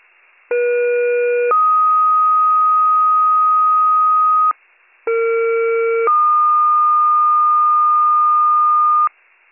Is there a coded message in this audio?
Yes: two-tone page from 0.5 to 9.1 s, heard as a one-second tone, then a three-second tone, about 480 then 1250 Hz, about 460 then 1220 Hz.